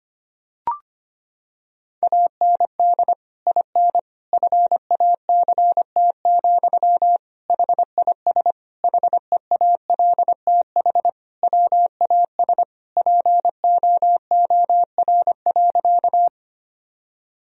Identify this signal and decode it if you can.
Morse code, heard as a keyed tone, about 25 wpm, 700 Hz. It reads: AND IN FACT, HIS HEALTH WAS POOR.
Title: Sentences from Top 1000 Words - Review Part 3 - 25wpm